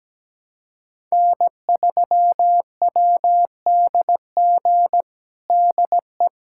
Morse code, heard as a keyed tone, about 17 wpm, 700 Hz. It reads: N3WDG DE
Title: USA Callsign DE Callsign - 5 Characters - 17wpm